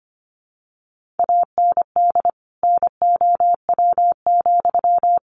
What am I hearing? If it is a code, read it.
Morse code, heard as a keyed tone, about 25 wpm, 700 Hz. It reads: AND NOW,
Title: Sentences from Top 1500 Words 25wpm